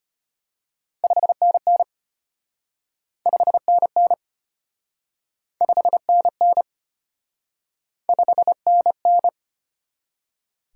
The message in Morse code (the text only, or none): 5NN 5NN 5NN 5NN